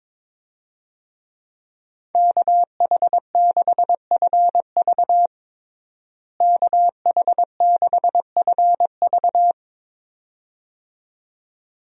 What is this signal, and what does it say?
Morse code, heard as a keyed tone, about 22 wpm, 700 Hz. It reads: KH6FV KH6FV
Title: USA Call Signs - 5 Characters - Repeated 2x - 22wpm